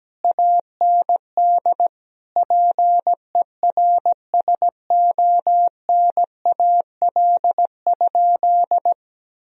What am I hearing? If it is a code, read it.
Morse code, heard as a keyed tone, about 17 wpm, 700 Hz. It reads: AND PERSONAL?
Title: Sentences from Top 1000 Words - Review Part 4 - 17wpm